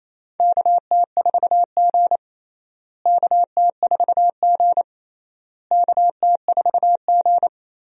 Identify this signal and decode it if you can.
Morse code, heard as a keyed tone, about 29 wpm, 700 Hz. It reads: KT4G KT4G KT4G
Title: USA Call Signs - 4 Characters - Repeated 3x - 28wpm